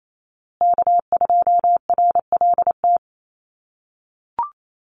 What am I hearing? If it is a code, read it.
Morse code, heard as a keyed tone, about 28 wpm, 700 Hz. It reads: K2RLT
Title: USA Call Signs Repeated 2x 28wpm